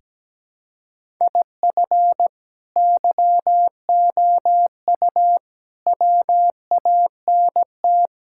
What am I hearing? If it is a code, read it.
Morse code, heard as a keyed tone, about 17 wpm, 700 Hz. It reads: IF YOU WANT
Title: Sentences from Top 1000 Words - Review Part 6 - 17wpm